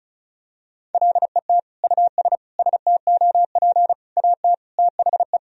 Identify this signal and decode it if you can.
Morse code, heard as a keyed tone, about 34 wpm, 700 Hz. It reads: LET US STOP AT THE
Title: Sentences from Top 400 Words and Abbreviations 35wpm